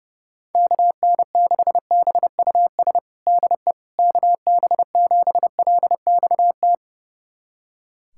Morse code, heard as a keyed tone, about 30 wpm, 700 Hz. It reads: KN6BUS DE KB7LXT
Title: USA Callsign DE Callsign - 6 Characters - 30wpm